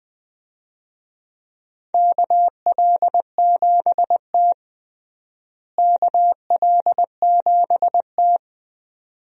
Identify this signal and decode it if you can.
Morse code, heard as a keyed tone, about 21 wpm, 700 Hz. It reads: KL7T KL7T
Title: USA Call Signs - 4 Characters - Repeated 2x - 20wpm